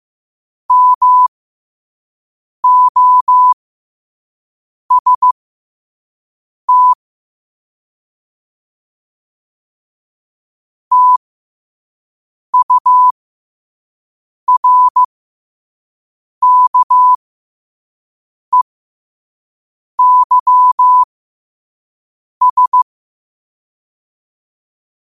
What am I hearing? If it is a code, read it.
Morse code, heard as a keyed tone, about 15 wpm, spaced as for 5 wpm, 1000 Hz. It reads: MOST TURKEYS
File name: qotdfiles / 5wpm